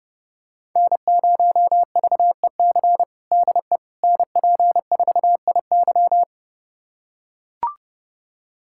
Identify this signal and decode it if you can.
Morse code, heard as a keyed tone, about 30 wpm, 700 Hz. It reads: N0VEC DE NP4IY
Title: USA Callsign DE Callsign - 5 Characters - 30wpm